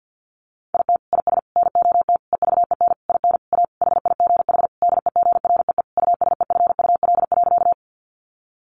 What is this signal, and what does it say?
Morse code, heard as a keyed tone, about 50 wpm, 700 Hz. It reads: IT IS NOT EVEN IN A HIGH DEGREE USEFUL.
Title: Sentences from Top 1500 Words 50wpm